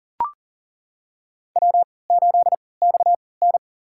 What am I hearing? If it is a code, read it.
Morse code, heard as a keyed tone, about 40 wpm, spaced as for 23 wpm, 700 Hz. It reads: W8XN
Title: USA Call Signs - 4 Characters - 3x Character Spacing - 40wpm